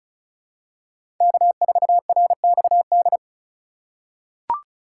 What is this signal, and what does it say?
Morse code, heard as a keyed tone, about 34 wpm, 700 Hz. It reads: K4RXD
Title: USA Call Signs - 5 Characters - 4x Character Spacing - 35wpm